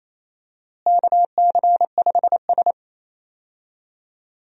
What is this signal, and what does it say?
Morse code, heard as a keyed tone, about 27 wpm, 700 Hz. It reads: KC5S
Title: USA Call Signs - 4 Characters - Repeated 2x - 28wpm